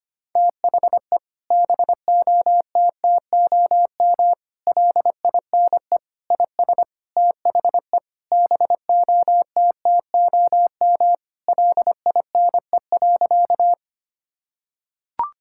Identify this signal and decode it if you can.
Morse code, heard as a keyed tone, about 25 wpm, 700 Hz. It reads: THE BOTTOM LINE IS THE BOTTOM LINE.
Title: Sentences from Top 1000 Words - Review Part 6 - 25wpm